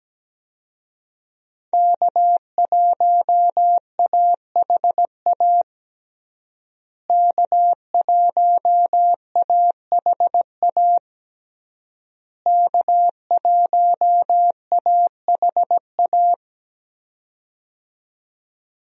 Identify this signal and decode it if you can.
Morse code, heard as a keyed tone, about 17 wpm, 700 Hz. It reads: K1AHA K1AHA K1AHA
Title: USA Call Signs - 5 Characters - Repeated 3x - 17wpm